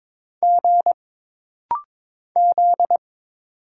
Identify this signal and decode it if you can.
Morse code, heard as a keyed tone, about 22 wpm, 700 Hz. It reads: G Z
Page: This rapid-fire timing is geared to push your ICR (Instant Character Recognition) skills.